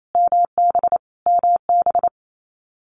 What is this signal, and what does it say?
Morse code, heard as a keyed tone, about 28 wpm, 700 Hz. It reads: MB MB
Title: Multi-letter Phonemes Repeated 3x 28wpm